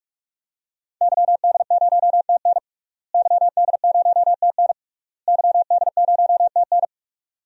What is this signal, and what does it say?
Morse code, heard as a keyed tone, about 41 wpm, 700 Hz. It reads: YD0TN YD0TN YD0TN
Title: DXCC-100 Call Signs - 5 Characters - Repeated 3x - 45wpm